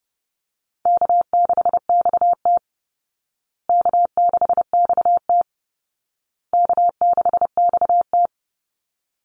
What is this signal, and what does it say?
Morse code, heard as a keyed tone, about 30 wpm, 700 Hz. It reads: K6XT K6XT K6XT
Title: Easy Call Signs Repeated 3x - 30wpm